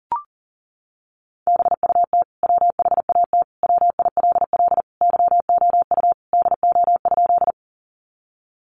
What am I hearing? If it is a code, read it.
Morse code, heard as a keyed tone, about 40 wpm, 700 Hz. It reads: BUT WHAT WILL YOU DO?
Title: Sentences from Top 100 Words 40wpm